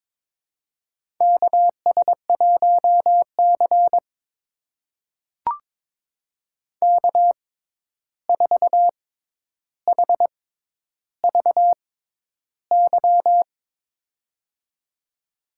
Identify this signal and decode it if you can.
Morse code, heard as a keyed tone, about 22 wpm, spaced as for 11 wpm, 700 Hz. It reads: KS1C K4HVY
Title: USA Call Signs - 6x Character Spacing - 22wpm